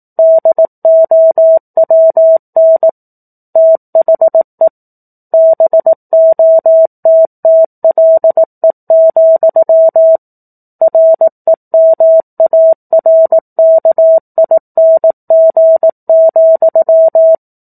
Morse code, heard as a keyed tone, about 18 wpm, 650 Hz. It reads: DOWN THE BOTTLE, REMARKING,
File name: 04-Chapter-04 18 WPM